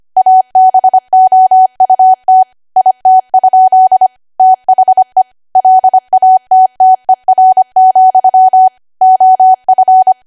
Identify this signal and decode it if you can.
Morse code, heard as a keyed tone, about 25 wpm, 750 Hz. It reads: ABOUT IT? THE LATTER, OF
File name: Archive / 25 WPM / 25WPM